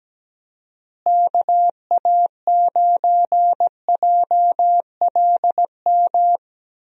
Morse code, heard as a keyed tone, about 17 wpm, 700 Hz. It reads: KA9JLM
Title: USA Call Signs - 6 Characters - Repeated 2x - 17wpm